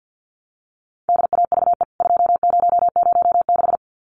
Morse code, heard as a keyed tone, about 50 wpm, 700 Hz. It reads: DAVE 2006
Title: CWT Exchange 50wpm